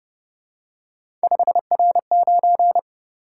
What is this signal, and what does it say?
Morse code, heard as a keyed tone, about 31 wpm, 700 Hz. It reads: 5R9
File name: Speed-Racing - Three Letter-Number Combinations - 25wpm